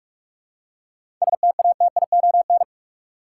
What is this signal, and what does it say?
Morse code, heard as a keyed tone, about 41 wpm, 700 Hz. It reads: STATION
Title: Seven Letter Words - 45wpm